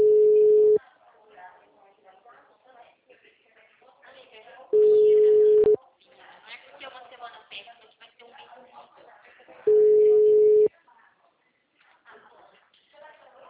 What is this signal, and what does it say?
Ringback tone (Europe, 425 Hz), with speech.
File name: out- -2002- - - .59254